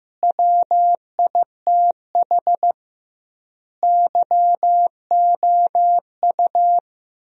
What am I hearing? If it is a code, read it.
Morse code, heard as a keyed tone, about 15 wpm, 700 Hz. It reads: WITH YOU
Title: 2x Word-Spacing - Sentences from Top 300 Words - 15wpm